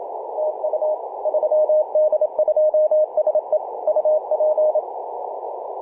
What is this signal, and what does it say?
Morse code, heard as a keyed tone, about 27 wpm, 620 Hz, with radio noise. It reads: TU 3D2SE UP
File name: 3d2se-18-080mhz-cw